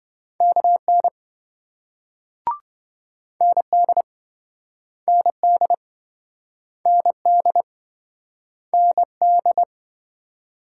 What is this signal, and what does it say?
Morse code, heard as a keyed tone, about 28 wpm, 700 Hz. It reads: KN ND ND ND ND